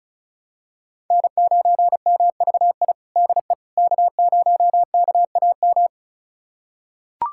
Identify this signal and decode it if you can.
Morse code, heard as a keyed tone, about 36 wpm, 700 Hz. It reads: N9MVI DE K0KAM
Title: USA Callsign DE Callsign - 5 Characters - 35wpm